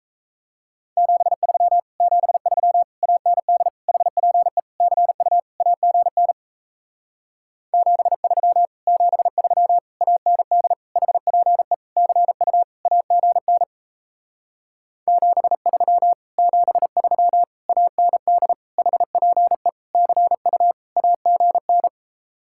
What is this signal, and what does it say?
Morse code, heard as a keyed tone, about 37 wpm, 700 Hz. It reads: 73 73 AND HPE CU AGN 73 73 AND HPE CU AGN 73 73 AND HPE CU AGN